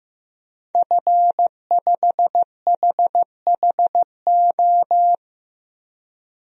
Morse code, heard as a keyed tone, about 15 wpm, 700 Hz. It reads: F5HHO
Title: DXCC-100 Call Signs - 5 Characters - 15wpm